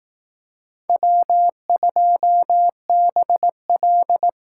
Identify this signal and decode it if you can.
Morse code, heard as a keyed tone, about 18 wpm, 700 Hz. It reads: W2BL
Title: Speed Racing - USA 1x2 Call Signs 15wpm